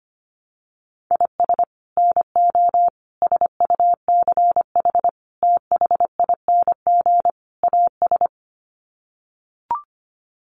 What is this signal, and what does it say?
Morse code, heard as a keyed tone, about 25 wpm, 700 Hz. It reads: IS NO SUCH THING AS
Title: Sets of 5 Words Encore 25wpm